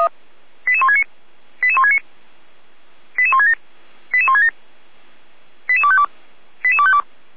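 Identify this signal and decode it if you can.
Five-tone selective call (ZVEI), heard as quick runs of five tones, tones of about 70 ms. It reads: 88169 88169 88167 88167 88252 88252
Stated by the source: Five-tone ZVEI selective call received by SDR.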